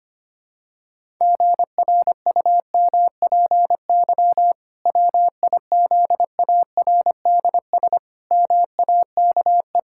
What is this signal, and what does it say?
Morse code, heard as a keyed tone, about 25 wpm, 700 Hz. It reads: GRUMPY WIZARDS MAKE
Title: Pangram Challenge v2 - 25wpm